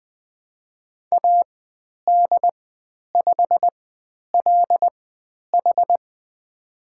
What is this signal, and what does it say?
Morse code, heard as a keyed tone, about 20 wpm, spaced as for 10 wpm, 700 Hz. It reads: AD5LH
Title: 095 Lesson 16g - Letter c - Call Signs 20wpm